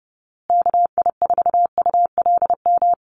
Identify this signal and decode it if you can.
Morse code, heard as a keyed tone, about 30 wpm, 700 Hz. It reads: KI4ULM
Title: USA Call Signs Repeated 3x 30wpm